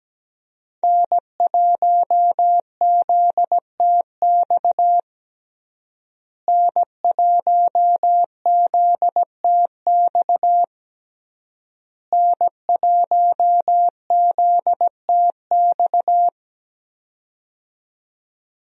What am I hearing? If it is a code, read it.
Morse code, heard as a keyed tone, about 17 wpm, 700 Hz. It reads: N1ZTX N1ZTX N1ZTX
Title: USA Call Signs - 5 Characters - Repeated 3x - 17wpm